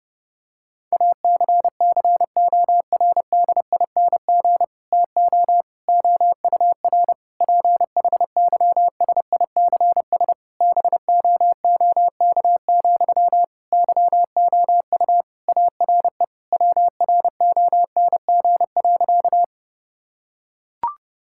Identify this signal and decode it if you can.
Morse code, heard as a keyed tone, about 30 wpm, 700 Hz. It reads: ACCORDING TO OUR PHYSICS BOOK, YOU ARE WRONG.
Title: Sentences from Top 1000 Words - Review Part 5 - 30wpm